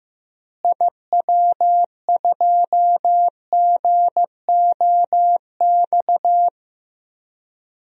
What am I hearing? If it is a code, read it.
Morse code, heard as a keyed tone, about 15 wpm, 700 Hz. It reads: IW2GOX
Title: DXCC-100 Call Signs - 6 Characters - Repeated 2x - 15wpm